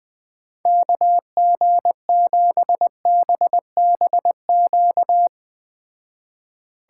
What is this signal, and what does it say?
Morse code, heard as a keyed tone, about 20 wpm, 700 Hz. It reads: KG7BBQ
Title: Speed Racing - USA 2x3 Call Signs 17wpm